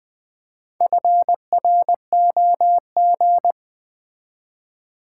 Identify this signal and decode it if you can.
Morse code, heard as a keyed tone, about 20 wpm, 700 Hz. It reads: FROG
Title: Build A Word - 2 Letters At A Time - 4-Letter Words - 20wpm